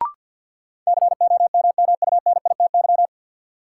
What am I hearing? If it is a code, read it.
Morse code, heard as a keyed tone, about 53 wpm, 700 Hz. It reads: COMMUNITY
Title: Sentences from Top 1200 Words - New Word Drill Alt - 50wpm